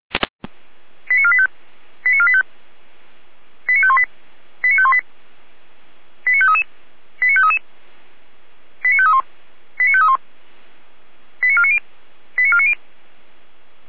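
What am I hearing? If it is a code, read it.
Five-tone selective call (ZVEI), heard as quick runs of five tones, tones of about 70 ms. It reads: 89485 89485 89518 89518 89533 89533 89531 89531 89490 89490
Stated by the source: ZVEI five-tone call 89020 received by SDR.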